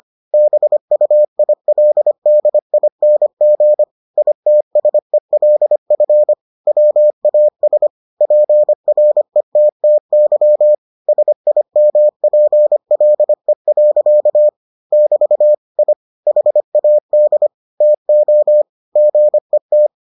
Morse code, heard as a keyed tone, about 25 wpm, 600 Hz. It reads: BUILDING ITSELF WAS PRETTY SIMPLE. <BT> I HAD TO GET